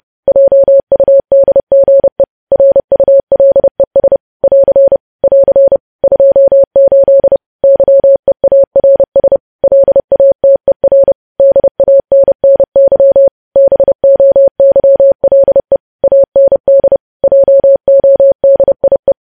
Morse code, heard as a keyed tone, about 30 wpm, 550 Hz. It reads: JUDGE RULES <AR> <AR> 28 YEARS LATER DANNY BOYLE AND JODIE
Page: news headlines in morse code at 30 wpm